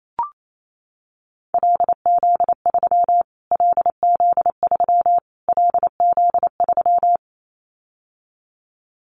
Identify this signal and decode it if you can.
Morse code, heard as a keyed tone, about 28 wpm, 700 Hz. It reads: LZ3 LZ3 LZ3
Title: Callsign Prefixes for Top 25 DXCC - Repeated 3x - 28wpm